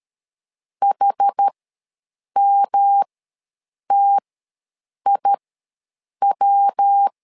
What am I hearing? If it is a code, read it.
Morse code, heard as a keyed tone, about 13 wpm, spaced as for 7 wpm, 800 Hz. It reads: HMTIW
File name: LEZIONI / 3 MWRO / 3 MW ws 12wpm 1816